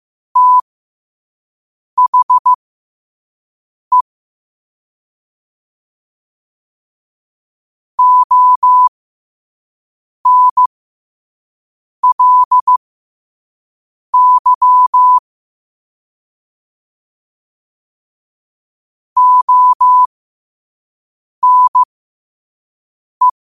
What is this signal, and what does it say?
Morse code, heard as a keyed tone, about 15 wpm, spaced as for 5 wpm, 1000 Hz. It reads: THE ONLY ONE
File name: qotdfiles / 5wpm